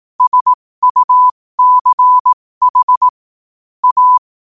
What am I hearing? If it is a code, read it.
Morse code, heard as a keyed tone, about 18 wpm, spaced as for 15 wpm, 1000 Hz. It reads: SUCH A
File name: qotdfiles / 15wpm